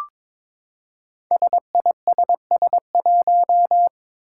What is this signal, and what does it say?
Morse code, heard as a keyed tone, about 21 wpm, 700 Hz. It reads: SISS1
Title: Five Challenging Letter-Number Combinations 22wpm